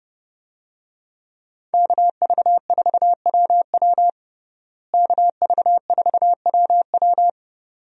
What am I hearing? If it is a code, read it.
Morse code, heard as a keyed tone, about 32 wpm, 700 Hz. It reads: KV4WW KV4WW
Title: USA Call Signs - 5 Characters - Repeated 2x - 30wpm